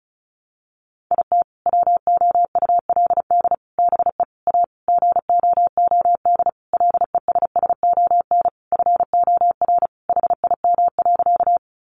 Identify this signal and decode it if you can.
Morse code, heard as a keyed tone, about 34 wpm, 700 Hz. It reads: IT WOULD BE A GOOD LESSON FOR HIM.
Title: Sentences from Top 1200 Words 35wpm